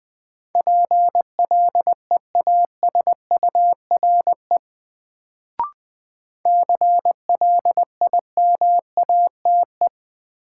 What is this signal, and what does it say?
Morse code, heard as a keyed tone, about 21 wpm, 700 Hz. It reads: PLEASURE CLIMATE
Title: Sentences from Top 1600 Words - New Word Drill Alt - 20wpm